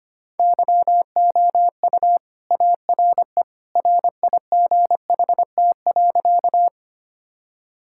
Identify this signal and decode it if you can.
Morse code, heard as a keyed tone, about 25 wpm, 700 Hz. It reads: YOU ARE RIGHT.
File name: practice / 25wpm / Sentences with 2-Word Phrases - 25wpm